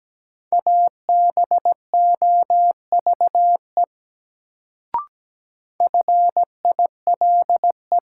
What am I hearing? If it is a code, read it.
Morse code, heard as a keyed tone, about 17 wpm, 700 Hz. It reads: ABOVE FILE
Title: Sentences from Top 1200 Words - New Word Drill Alt - 17wpm